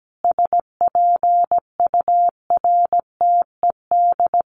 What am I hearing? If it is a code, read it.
Morse code, heard as a keyed tone, about 17 wpm, 700 Hz. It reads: SPURTED
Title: A Sound Of Thunder 17wpm